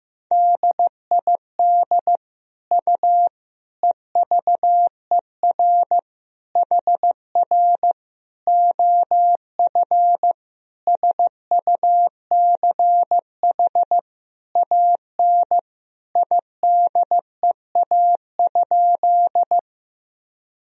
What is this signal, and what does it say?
Morse code, heard as a keyed tone, about 15 wpm, 700 Hz. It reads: DID U EVER HR OF SUCH AN IDEA?
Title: Sentences from Top 400 Words and Abbreviations 15wpm